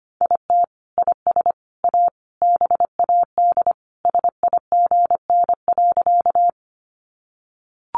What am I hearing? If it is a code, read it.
Morse code, heard as a keyed tone, about 25 wpm, 700 Hz. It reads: IT IS A BAD SIGN.
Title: Sentences from Top 1000 Words 25wpm